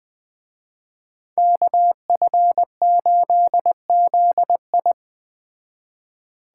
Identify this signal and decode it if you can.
Morse code, heard as a keyed tone, about 21 wpm, 700 Hz. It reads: KF8ZI
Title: Speed Racing - USA Call Signs 17wpm